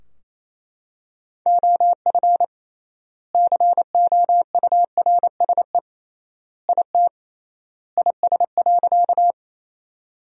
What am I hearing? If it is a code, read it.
Morse code, heard as a keyed tone, about 28 wpm, 700 Hz. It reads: OF COURSE IT IS.
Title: 3x Word-Spacing - Sentences from Top 500 Words - 28wpm